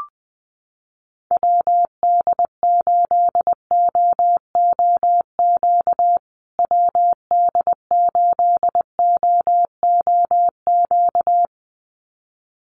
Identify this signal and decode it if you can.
Morse code, heard as a keyed tone, about 20 wpm, 700 Hz. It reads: WD8OOQ WD8OOQ
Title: USA Call Signs Repeated 2x 20wpm